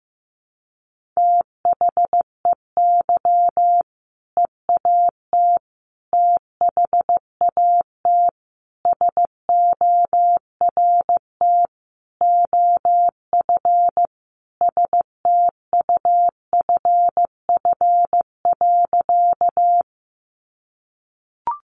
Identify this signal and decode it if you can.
Morse code, heard as a keyed tone, about 15 wpm, 700 Hz. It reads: THEY EAT THAT SORT OF STUFF.
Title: Sentences from Top 1400 Words 15wpm